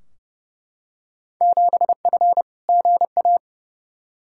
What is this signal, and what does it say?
Morse code, heard as a keyed tone, about 29 wpm, 700 Hz. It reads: 7F GA
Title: Field Day Exchanges Encore - 30wpm